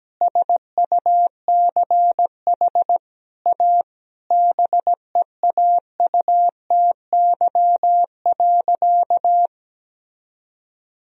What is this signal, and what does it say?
Morse code, heard as a keyed tone, about 17 wpm, 700 Hz. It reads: SUCH A BEAUTY.
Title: Sentences from Top 1000 Words - Review Part 2 - 17wpm